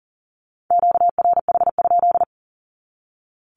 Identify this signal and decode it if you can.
Morse code, heard as a keyed tone, about 40 wpm, 700 Hz. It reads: QRH?
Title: Q-Codes 40wpm